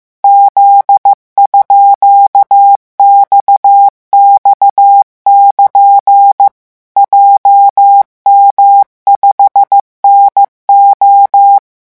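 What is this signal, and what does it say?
Morse code, heard as a keyed tone, about 15 wpm, 800 Hz. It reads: Z_XX<KN> JM5NO
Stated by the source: Morse revision lesson at 14 words per minute.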